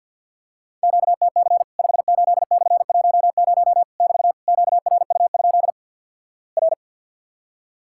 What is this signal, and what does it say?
Morse code, heard as a keyed tone, about 50 wpm, 700 Hz. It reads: QTC 58/10 <BT> QRU? R
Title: Worked All Europe DX Contest - QTC Exchange - 50wpm